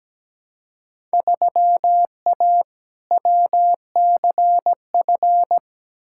Morse code, heard as a keyed tone, about 17 wpm, 700 Hz. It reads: 3A WCF
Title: Field Day Exchanges Encore - 17wpm